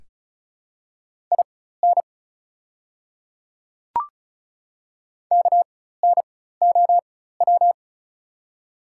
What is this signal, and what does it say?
Morse code, heard as a keyed tone, about 36 wpm, spaced as for 16 wpm, 690 Hz. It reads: IN KNOW
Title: 4x Character Spacing - Top 100 Words - 35wpm